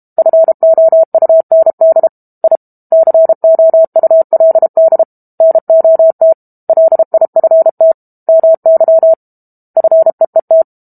Morse code, heard as a keyed tone, about 32 wpm, 650 Hz. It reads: FOUND I COULD NOT LIFT MY FEET
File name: Book Two / 06-Chapter 6 - 32 WPM